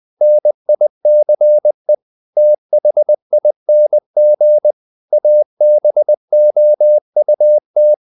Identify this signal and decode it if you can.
Morse code, heard as a keyed tone, about 20 wpm, 600 Hz. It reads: NICE THING ABOUT
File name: CW / Practice / EnglishQuotations0320